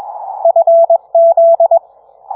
Morse code, heard as a keyed tone, about 21 wpm, 670 Hz. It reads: FZ